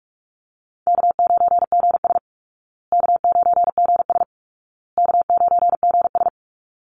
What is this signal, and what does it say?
Morse code, heard as a keyed tone, about 44 wpm, 700 Hz. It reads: K9GS K9GS K9GS
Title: Easy Call Signs Repeated 3x - 45wpm